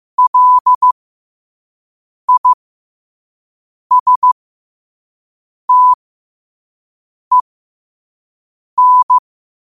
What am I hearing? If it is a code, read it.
Morse code, heard as a keyed tone, about 15 wpm, spaced as for 5 wpm, 1000 Hz. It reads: LISTEN